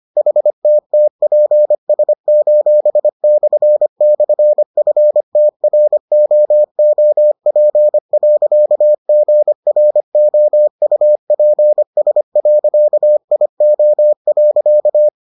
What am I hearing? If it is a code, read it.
Morse code, heard as a keyed tone, about 25 wpm, 600 Hz. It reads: HTTPS://FTROOP.GROUPS.IO.